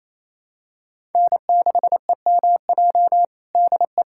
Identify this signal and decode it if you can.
Morse code, heard as a keyed tone, about 28 wpm, 700 Hz. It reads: N6EMJ DE
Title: USA Callsign DE Callsign 28wpm